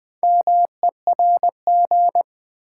Morse code, heard as a keyed tone, about 20 wpm, 700 Hz. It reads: MERG
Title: Build A Word - 1 Letter At A Time - 5-Letter Words - 20wpm